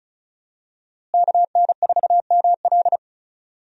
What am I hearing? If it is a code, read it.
Morse code, heard as a keyed tone, about 36 wpm, 700 Hz. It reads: KN4ML
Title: USA Call Signs - 5 Characters - Repeated 2x - 35wpm